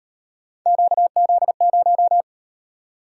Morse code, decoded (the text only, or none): QZ0